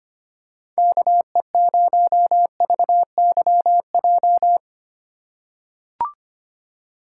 Morse code, heard as a keyed tone, about 25 wpm, 700 Hz. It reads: KE0VYJ
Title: USA Call Signs - 6 Characters - 3x Character Spacing - 25wpm